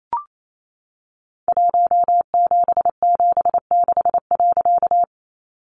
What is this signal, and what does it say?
Morse code, heard as a keyed tone, about 28 wpm, 700 Hz. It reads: 1776.
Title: A Sound Of Thunder 28wpm